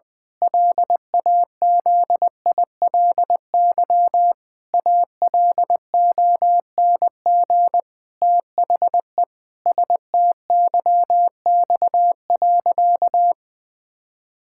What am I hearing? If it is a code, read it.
Morse code, heard as a keyed tone, about 20 wpm, 700 Hz. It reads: LAZILY ALONG THE STYX.
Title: Pangram Challenge v2 - 20wpm